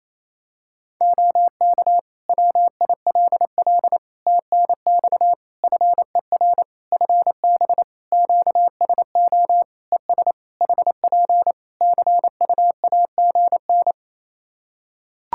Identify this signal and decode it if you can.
Morse code, heard as a keyed tone, about 27 wpm, 700 Hz. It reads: OK WILL TNX FER FB QSO ES HP CUAGN
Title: Practice QSO - Exchange 4 - Ending - 28wpm